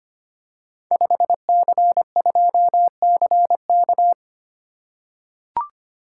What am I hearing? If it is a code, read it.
Morse code, heard as a keyed tone, about 25 wpm, 700 Hz. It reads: 5C2CK
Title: Code Groups - 5 Letter:Numbers - 25wpm